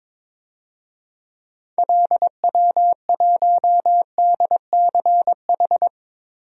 Morse code, heard as a keyed tone, about 22 wpm, 700 Hz. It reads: LW1DCH
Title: DXCC-100 Call Signs - 6 Characters - Repeated 2x - 22wpm